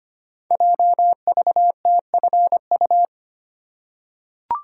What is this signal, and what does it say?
Morse code, heard as a keyed tone, about 25 wpm, 700 Hz. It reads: JVTFU
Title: Code Groups - 5 Letter:Numbers - 25wpm